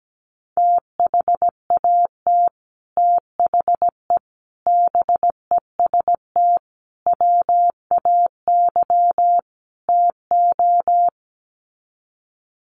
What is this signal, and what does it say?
Morse code, heard as a keyed tone, about 17 wpm, 700 Hz. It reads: THAT THE BEST WAY TO
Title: Sets of 5 Words Encore2 17wpm